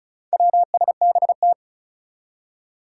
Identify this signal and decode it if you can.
Morse code, heard as a keyed tone, about 36 wpm, 700 Hz. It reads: WSBT
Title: Four Challenging Letter-Number Combinations 35wpm